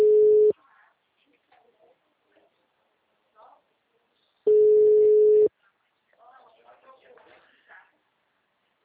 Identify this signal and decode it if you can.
Ringback tone (Europe, 425 Hz), with speech.
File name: out- -2017- - - .48052